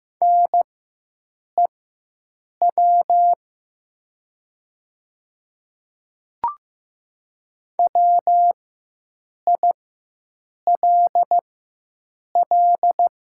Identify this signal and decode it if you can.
Morse code, heard as a keyed tone, about 15 wpm, spaced as for 7 wpm, 700 Hz. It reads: NEW WILL
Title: 4x Character Spacing - Top 200 Words - 15wpm